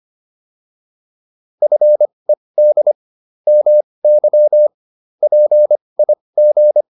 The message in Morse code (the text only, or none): FED MY PIG